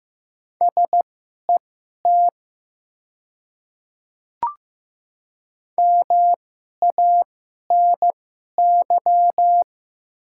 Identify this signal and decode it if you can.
Morse code, heard as a keyed tone, about 15 wpm, spaced as for 11 wpm, 700 Hz. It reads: SET MANY
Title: 2x Character Spacing - Top 200 Words - 15wpm